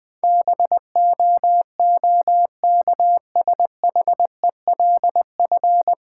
Morse code, heard as a keyed tone, about 20 wpm, 690 Hz. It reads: BOOKSHELF
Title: Difficult Words - Speed-Racing with Hints - 20wpm